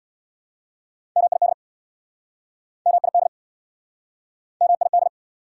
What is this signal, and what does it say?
Morse code, heard as a keyed tone, about 69 wpm, 690 Hz. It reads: MID MID MID